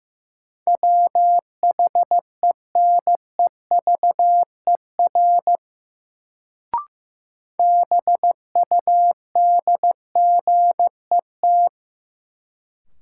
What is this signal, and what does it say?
Morse code, heard as a keyed tone, about 15 wpm, 700 Hz. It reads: WHENEVER BUDGET
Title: Sentences from Top 1200 Words - New Word Drill - 15wpm